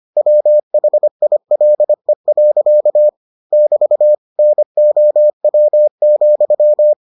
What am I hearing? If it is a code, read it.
Morse code, heard as a keyed tone, about 25 wpm, 600 Hz. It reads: WHILE. <BT> NOW,